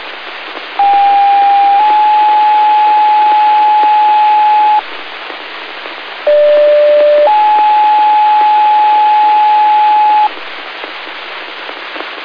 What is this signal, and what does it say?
Two-tone page, heard as a one-second tone, then a three-second tone, about 800 then 850 Hz, about 590 then 850 Hz, with noise. It is heard from 0.8 to 10.3 s.